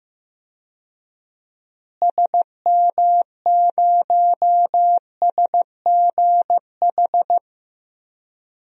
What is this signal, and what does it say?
Morse code, heard as a keyed tone, about 16 wpm, 700 Hz. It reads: SM0SGH
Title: DXCC-100 Call Signs - 6 Characters - Repeated 2x - 15wpm